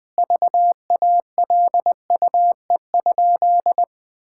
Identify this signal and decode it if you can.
Morse code, heard as a keyed tone, about 20 wpm, 700 Hz. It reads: VALUE?
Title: Sentences from Top 1000 Words - Review Part 6 - 20wpm